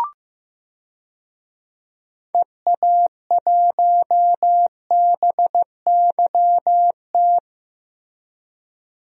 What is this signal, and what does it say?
Morse code, heard as a keyed tone, about 15 wpm, 700 Hz. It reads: EA1BYT
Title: DXCC-100 Call Signs - 6 Characters - Repeated 3x - 15wpm